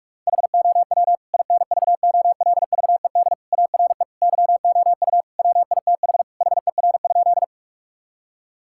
Morse code, heard as a keyed tone, about 46 wpm, 700 Hz. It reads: HOW INVOLVED ARE YOU WITH HER?
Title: Sentences from Top 1000 Words - Review Part 5 - 45wpm